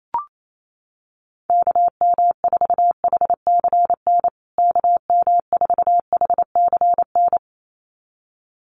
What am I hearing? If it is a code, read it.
Morse code, heard as a keyed tone, about 28 wpm, 700 Hz. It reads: KM4HCN KM4HCN
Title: USA Call Signs Repeated 2x 28wpm